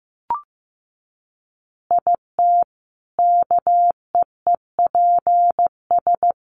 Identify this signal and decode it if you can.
Morse code, heard as a keyed tone, about 15 wpm, 700 Hz. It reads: IT KEEPS
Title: Additional Easy Phrases 15wpm